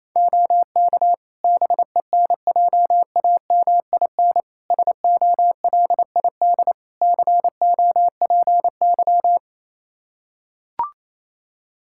Morse code, heard as a keyed tone, about 28 wpm, 700 Hz. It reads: OK BENJAMIN SOLID COPY
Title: K1USN SST QSO Practice 28wpm